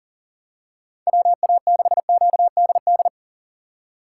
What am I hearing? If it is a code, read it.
Morse code, heard as a keyed tone, about 40 wpm, 700 Hz. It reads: WA6QDD
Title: USA Call Signs - 6 Characters - 4x Character Spacing - 40wpm